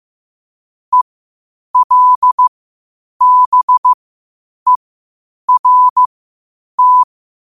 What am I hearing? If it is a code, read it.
Morse code, heard as a keyed tone, about 15 wpm, spaced as for 8 wpm, 1000 Hz. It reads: ELBERT
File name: qotdfiles / 8wpm